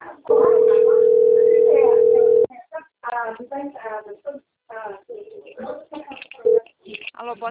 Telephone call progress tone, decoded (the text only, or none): ringback tone